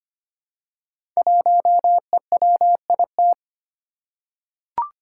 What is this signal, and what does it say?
Morse code, heard as a keyed tone, about 25 wpm, 700 Hz. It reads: 1EWIT
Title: Five Challenging Letter-Number Combinations 25wpm